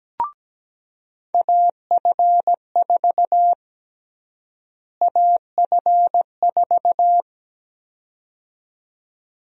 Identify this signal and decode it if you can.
Morse code, heard as a keyed tone, about 17 wpm, 700 Hz. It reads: AF4 AF4
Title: USA Call Sign Prefixes - Repeated 2x - 17wpm